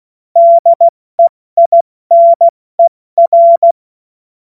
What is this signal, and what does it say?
Morse code, heard as a keyed tone, about 16 wpm, 680 Hz. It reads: DEINER
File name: Morsekurs / Witze / Witz 09. August 2022 70 F80